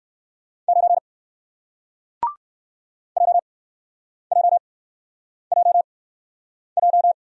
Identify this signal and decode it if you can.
Morse code, heard as a keyed tone, about 56 wpm, spaced as for 9 wpm, 700 Hz. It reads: 9 JJJJ
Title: Speed-Racing Single Letter-Number 45wpm